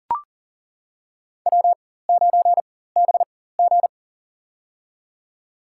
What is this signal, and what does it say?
Morse code, heard as a keyed tone, about 40 wpm, spaced as for 19 wpm, 700 Hz. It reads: W9BG
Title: USA Call Signs - 4x Character Spacing - 40wpm